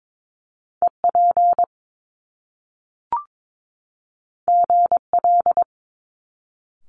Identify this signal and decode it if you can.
Morse code, heard as a keyed tone, about 22 wpm, 700 Hz. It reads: EP GL
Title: Top Two Letter Combinations in Words 22wpm